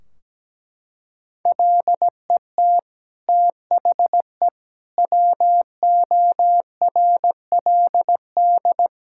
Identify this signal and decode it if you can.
Morse code, heard as a keyed tone, about 17 wpm, 700 Hz. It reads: LET THE WORLD
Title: Sentences from Top 1600 Words 17wpm